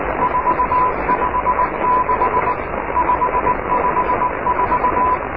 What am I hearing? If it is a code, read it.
Morse code, heard as a keyed tone, about 18 wpm, 1000 Hz, with radio noise. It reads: 456 554
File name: oscar 7 de k5dze section 2